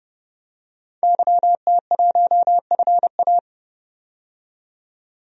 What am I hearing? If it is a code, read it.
Morse code, heard as a keyed tone, about 30 wpm, 700 Hz. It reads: YT1FA
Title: DXCC-100 Call Signs - 5 Characters - 30wpm